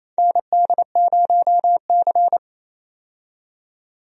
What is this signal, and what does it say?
Morse code, heard as a keyed tone, about 27 wpm, 700 Hz. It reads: ND0C
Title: USA Call Signs - 4 Characters - 28wpm